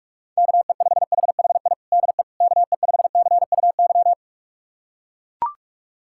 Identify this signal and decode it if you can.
Morse code, heard as a keyed tone, about 41 wpm, 690 Hz. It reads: KE5HHI DE KE5CUY
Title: USA Callsign DE Callsign - 6 Characters - 45wpm